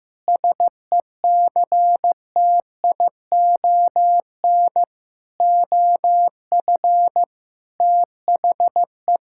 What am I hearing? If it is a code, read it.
Morse code, heard as a keyed tone, about 16 wpm, 700 Hz. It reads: SECTION OF THE
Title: Sentences from Top 1000 Words - Review Part 5 - 15wpm